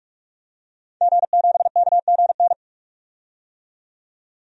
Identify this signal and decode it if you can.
Morse code, heard as a keyed tone, about 44 wpm, 690 Hz. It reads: G7KGN
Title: DXCC-100 Call Signs - 5 Characters - 45wpm